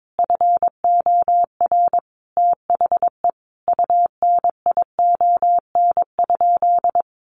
Morse code, heard as a keyed tone, about 22 wpm, 700 Hz. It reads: FOR THE UNION?
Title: Sentences from Top 1400 Words 22wpm